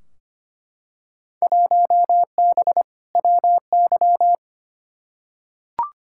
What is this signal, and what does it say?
Morse code, heard as a keyed tone, about 25 wpm, 700 Hz. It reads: 1B WY
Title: Field Day Exchanges Encore - 25wpm